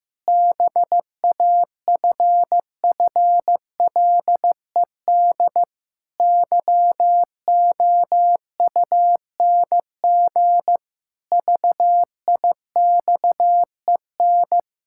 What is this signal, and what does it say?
Morse code, heard as a keyed tone, about 15 wpm, 700 Hz. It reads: BAFFLED YOUNG VIXEN
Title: Pangram Challenge v2 - 15wpm